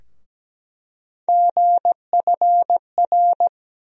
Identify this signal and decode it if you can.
Morse code, heard as a keyed tone, about 17 wpm, 700 Hz. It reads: GFR
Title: USA Call Sign Postfixes - 3 Letters 17wpm